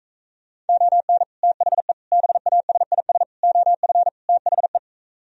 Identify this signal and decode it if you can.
Morse code, heard as a keyed tone, about 36 wpm, 700 Hz. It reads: ON THE BASIS OF THE